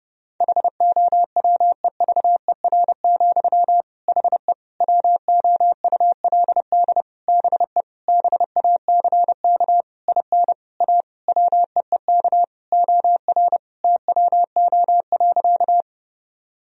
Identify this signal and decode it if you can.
Morse code, heard as a keyed tone, about 30 wpm, 700 Hz. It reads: HOWEVER, HE WOULD BE BACK IN A WEEK OR TWO.
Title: Sentences from Top 1000 Words - Review Part 2 - 30wpm